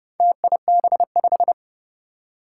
Morse code, heard as a keyed tone, about 30 wpm, 700 Hz. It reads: TIB5
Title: Four Challenging Letter-Number Combinations 30wpm